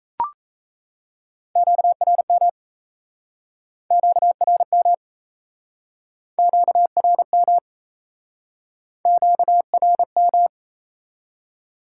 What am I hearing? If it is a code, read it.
Morse code, heard as a keyed tone, about 37 wpm, 700 Hz. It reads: QRM QRM QRM QRM